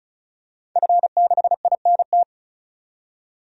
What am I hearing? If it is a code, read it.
Morse code, heard as a keyed tone, about 36 wpm, 700 Hz. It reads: F6INT
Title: DXCC-100 Call Signs - 5 Characters - 35wpm